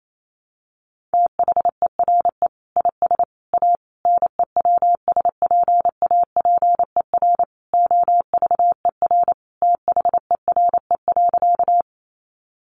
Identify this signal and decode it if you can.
Morse code, heard as a keyed tone, about 28 wpm, 700 Hz. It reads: THERE IS A NEWSPAPER OVER THERE.
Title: Sentences from Top 1100 Words 28wpm